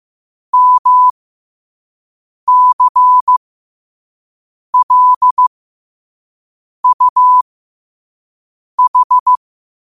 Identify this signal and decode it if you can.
Morse code, heard as a keyed tone, about 15 wpm, spaced as for 5 wpm, 1000 Hz. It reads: MCLUH
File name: qotdfiles / 5wpm